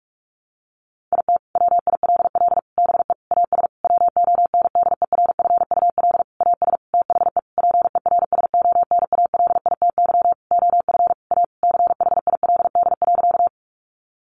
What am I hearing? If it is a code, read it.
Morse code, heard as a keyed tone, about 44 wpm, 700 Hz. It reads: IT WILL BE AS WONDERFUL AS THE PERSONALITY OF A CHILD.
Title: Sentences from Top 1400 Words 45wpm